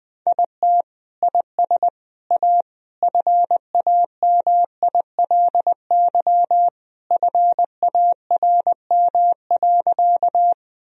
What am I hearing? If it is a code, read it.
Morse code, heard as a keyed tone, about 19 wpm, 700 Hz. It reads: IT IS A FAMILY FARM.
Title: Sentences from Top 1000 Words - Review Part 2 - 20wpm